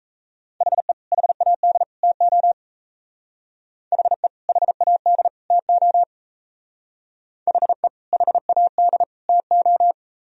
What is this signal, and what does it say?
Morse code, heard as a keyed tone, about 36 wpm, 700 Hz. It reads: HE HAD TO HE HAD TO HE HAD TO